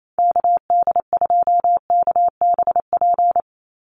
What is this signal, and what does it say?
Morse code, heard as a keyed tone, about 28 wpm, 700 Hz. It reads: KD2KBP
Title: USA Call Signs 28wpm